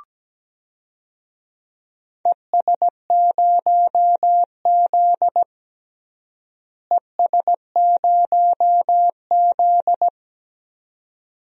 Morse code, heard as a keyed tone, about 17 wpm, 700 Hz. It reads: ES0Z ES0Z
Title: DXCC-100 Call Signs - 4 Characters - Repeated 2x - 17wpm